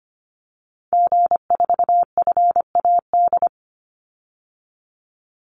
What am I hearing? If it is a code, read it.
Morse code, heard as a keyed tone, about 25 wpm, 700 Hz. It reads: G4FAD
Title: Call Signs Repeated 2x - 25wpm